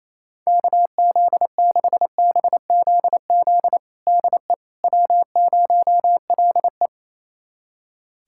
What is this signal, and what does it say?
Morse code, heard as a keyed tone, about 28 wpm, 700 Hz. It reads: KZ6BZZ DE W0LE
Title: USA Callsign DE Callsign 28wpm